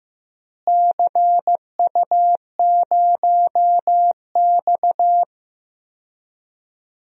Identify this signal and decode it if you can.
Morse code, heard as a keyed tone, about 16 wpm, 700 Hz. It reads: CU0X
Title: DXCC-100 Call Signs - 4 Characters - Repeated 2x - 15wpm